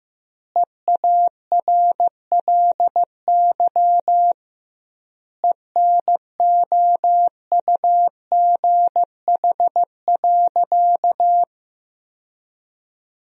Morse code, heard as a keyed tone, about 15 wpm, 700 Hz. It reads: EARLY ENOUGH.
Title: 2x Word-Spacing - Sentences from Top 400 Words - 15wpm